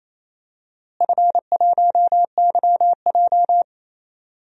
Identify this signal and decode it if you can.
Morse code, heard as a keyed tone, about 28 wpm, 700 Hz. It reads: F1YJ
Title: DXCC-100 Call Signs - 4 Characters - Repeated 2x - 28wpm